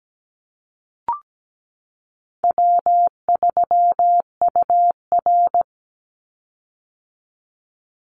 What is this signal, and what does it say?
Morse code, heard as a keyed tone, about 17 wpm, 700 Hz. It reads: W3UR
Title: Easy Call Signs - 17wpm